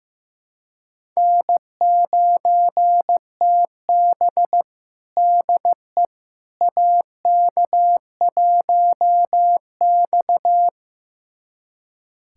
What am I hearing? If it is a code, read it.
Morse code, heard as a keyed tone, about 15 wpm, 700 Hz. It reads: N9TB DE AK1X
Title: USA Callsign DE Callsign - 4 Characters - 15wpm